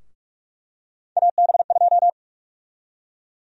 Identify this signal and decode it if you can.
Morse code, heard as a keyed tone, about 43 wpm, 700 Hz. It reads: AB2
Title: USA Call Sign Prefixes - 45wpm